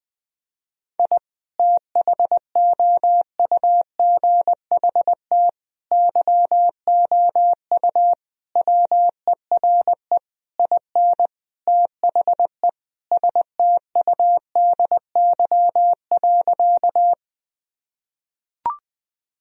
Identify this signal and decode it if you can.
Morse code, heard as a keyed tone, about 20 wpm, 700 Hz. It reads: I THOUGHT YOU WERE IN THE STUDY.
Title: Sentences from Top 1000 Words - Review Part 1 - 20wpm